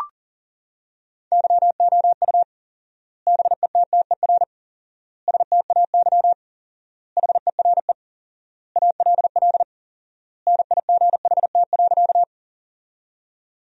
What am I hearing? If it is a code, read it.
Morse code, heard as a keyed tone, about 40 wpm, 700 Hz. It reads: YOU BETTER STAY HERE ALL NIGHT.
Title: 4x Word-Spacing - Sentences from Top 500 Words - 40wpm